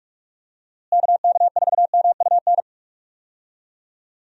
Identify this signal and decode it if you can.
Morse code, heard as a keyed tone, about 42 wpm, 700 Hz. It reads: KK4MUN
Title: USA Call Signs - 6 Characters - Repeated 2x - 45wpm